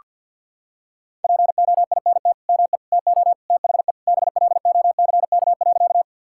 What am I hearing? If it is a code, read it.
Morse code, heard as a keyed tone, about 50 wpm, 700 Hz. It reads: POINT ME TO THE BLOCK.
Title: Sentences from Top 1600 Words 50wpm